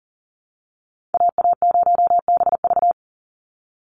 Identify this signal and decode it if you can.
Morse code, heard as a keyed tone, about 40 wpm, 700 Hz. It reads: AA0BV
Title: NA SOTA Activator Callsigns 40wpm